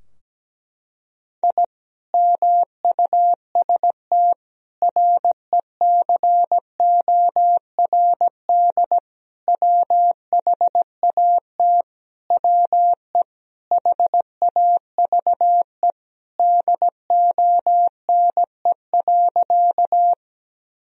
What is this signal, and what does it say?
Morse code, heard as a keyed tone, about 17 wpm, 700 Hz. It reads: I MUST RECORD WHAT WE HAVE DONE.
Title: Sentences from Top 1000 Words - Review Part 3 - 17wpm